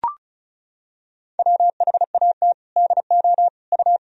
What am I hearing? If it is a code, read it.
Morse code, heard as a keyed tone, about 34 wpm, 700 Hz. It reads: WHAT DO U
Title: Sentences from Top 100 Words and Abbreviations 35wpm